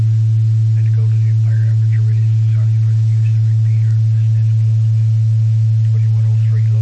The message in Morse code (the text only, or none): none